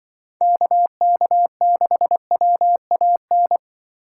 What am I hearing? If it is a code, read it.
Morse code, heard as a keyed tone, about 24 wpm, 700 Hz. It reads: KK6WAN
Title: Speed Racing - USA Call Signs 20wpm